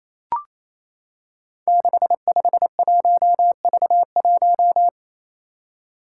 Morse code, heard as a keyed tone, about 28 wpm, 700 Hz. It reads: 651V1
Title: Five Challenging Letter-Number Combinations 28wpm